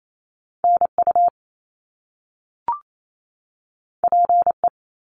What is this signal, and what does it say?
Morse code, heard as a keyed tone, about 28 wpm, 700 Hz. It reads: NU PE
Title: Canadian Provinces and Territories 28wpm